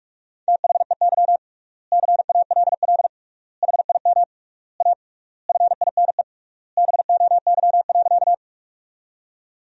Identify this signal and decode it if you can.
Morse code, heard as a keyed tone, about 46 wpm, 700 Hz. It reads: THEY CALL HIM A FINE BOY.
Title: 3x Word-Spacing - Sentences from Top 500 Words - 45wpm